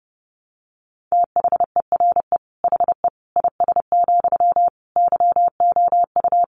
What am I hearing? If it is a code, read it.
Morse code, heard as a keyed tone, about 30 wpm, 700 Hz. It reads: THERE HE IS, YOU
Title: Sentences from Top 100 Words 30wpm